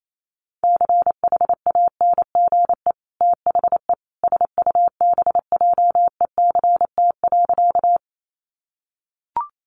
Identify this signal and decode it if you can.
Morse code, heard as a keyed tone, about 28 wpm, 700 Hz. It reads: CHANGE THE SUBJECT.
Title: Sentences from Top 800 Words 28wpm